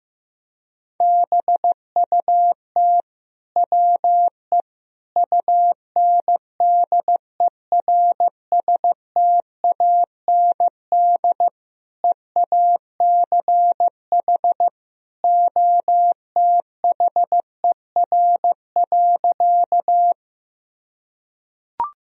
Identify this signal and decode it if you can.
Morse code, heard as a keyed tone, about 15 wpm, 700 Hz. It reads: BUT WE UNDERSTAND EACH OTHER.
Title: Sentences from Top 1000 Words - Review Part 1 - 15wpm